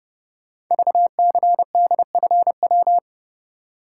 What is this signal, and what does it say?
Morse code, heard as a keyed tone, about 30 wpm, 700 Hz. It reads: VCDFW
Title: Code Groups - 5 Letters - 30wpm